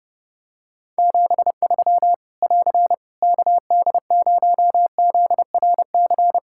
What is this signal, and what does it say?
Morse code, heard as a keyed tone, about 30 wpm, 700 Hz. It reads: 73 <AR> KD0ZRC
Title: Practice QSO - Exchange 4 - Ending - 30wpm